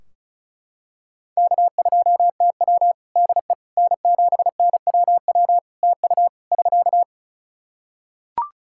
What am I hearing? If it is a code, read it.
Morse code, heard as a keyed tone, about 36 wpm, 700 Hz. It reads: K2TW DE N7NWW TU <SK>
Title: K1USN SST QSO Practice 35wpm